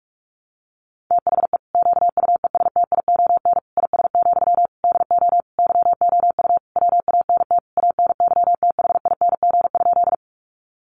Morse code, heard as a keyed tone, about 44 wpm, 700 Hz. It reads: THE QUESTION IS, DO YOU WANT ANYTHING?
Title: Sentences from Top 1000 Words 45wpm